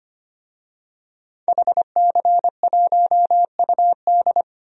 Morse code, heard as a keyed tone, about 25 wpm, 700 Hz. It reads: HC1UD
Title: DXCC-100 Call Signs - 5 Characters - Repeated 3x - 25wpm